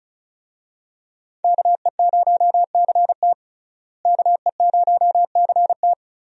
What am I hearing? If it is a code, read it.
Morse code, heard as a keyed tone, about 34 wpm, 700 Hz. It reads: KE0CT KE0CT
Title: USA Call Signs - 5 Characters - Repeated 3x - 35wpm